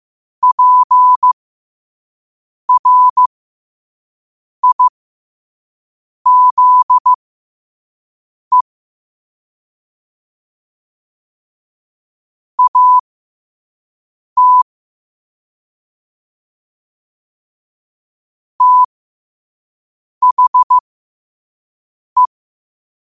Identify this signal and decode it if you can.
Morse code, heard as a keyed tone, about 15 wpm, spaced as for 5 wpm, 1000 Hz. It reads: PRIZE AT THE
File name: 5wpm